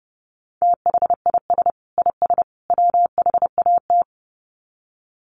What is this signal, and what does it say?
Morse code, heard as a keyed tone, about 30 wpm, 700 Hz. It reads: THIS IS WHAT
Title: Sets of 3 Words 30wpm